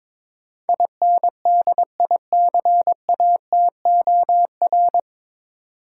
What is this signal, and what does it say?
Morse code, heard as a keyed tone, about 22 wpm, 700 Hz. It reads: INDICATOR
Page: Morse Code Ninja practice 22wpm